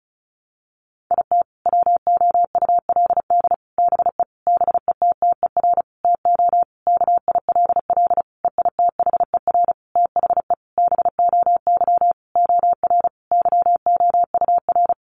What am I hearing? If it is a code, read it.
Morse code, heard as a keyed tone, about 34 wpm, 700 Hz. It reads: IT WOULD BE BETTER TO KILL EITHER THE BOY OR YOUR
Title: Sentences from Top 1500 Words 35wpm